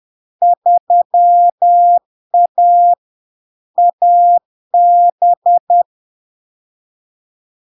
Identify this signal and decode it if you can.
Morse code, heard as a keyed tone, about 10 wpm, 700 Hz. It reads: 3A AB
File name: LARK-FD-2022-Log-in-CW-10wpm-700-Hz